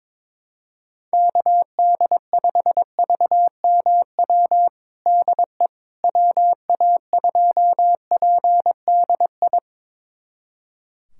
Morse code, heard as a keyed tone, about 22 wpm, 700 Hz. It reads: KD5VMW DE WA2PDI
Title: USA Callsign DE Callsign - 6 Characters - 22wpm